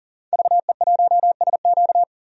Morse code, heard as a keyed tone, about 37 wpm, 700 Hz. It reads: VE1SQ
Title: DXCC-100 Call Signs - 5 Characters - 40wpm